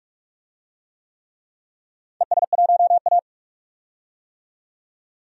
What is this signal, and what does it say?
Morse code, heard as a keyed tone, about 46 wpm, 700 Hz. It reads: ES1A
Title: DXCC-100 Call Signs - 4 Characters - 45wpm